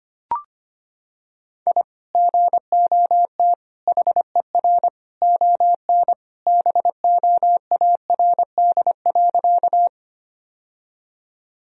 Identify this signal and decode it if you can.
Morse code, heard as a keyed tone, about 25 wpm, 700 Hz. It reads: I GOT HER ON BOARD.
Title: Sentences from Top 1000 Words - Review Part 4 - 25wpm